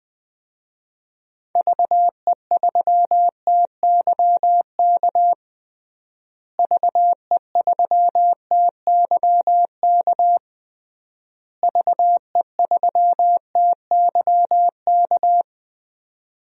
Morse code, heard as a keyed tone, about 21 wpm, 700 Hz. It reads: VE3TYK VE3TYK VE3TYK
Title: DXCC-100 Call Signs - 6 Characters - Repeated 3x - 20wpm